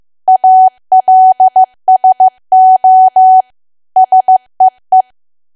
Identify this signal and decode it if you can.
Morse code, heard as a keyed tone, about 15 wpm, 750 Hz. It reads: ALSO SEE
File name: cwa / W1AW-Level2 / 15WPM